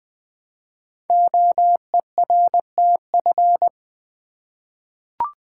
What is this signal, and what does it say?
Morse code, heard as a keyed tone, about 20 wpm, 700 Hz. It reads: OERTF
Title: Code Groups - 5 Letters - 20wpm